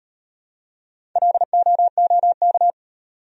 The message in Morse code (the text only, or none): LOOK